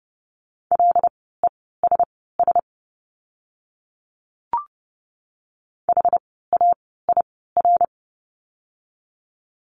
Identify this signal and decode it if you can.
Morse code, heard as a keyed tone, about 30 wpm, spaced as for 17 wpm, 700 Hz. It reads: LESS HAIR
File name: Lesson 17f - Letter u - Words 30wpm